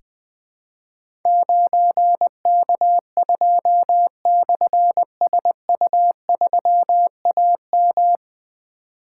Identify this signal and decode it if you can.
Morse code, heard as a keyed tone, about 20 wpm, 700 Hz. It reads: 9K2/SU3AM
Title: Difficult Call Signs 20wpm